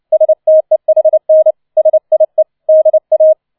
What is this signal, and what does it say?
Morse code, heard as a keyed tone, about 29 wpm, 630 Hz. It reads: STEHN SIE DA
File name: 006 Sechster Streich 30wpm